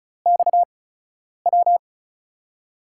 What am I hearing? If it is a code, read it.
Morse code, heard as a keyed tone, about 36 wpm, spaced as for 10 wpm, 700 Hz. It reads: XW
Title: USA Call Signs - 5 Characters - 8x Character Spacing - 35wpm